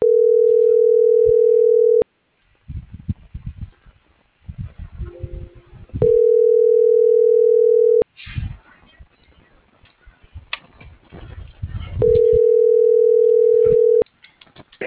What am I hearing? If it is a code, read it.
Ringback tone (North America), 440 + 480 Hz.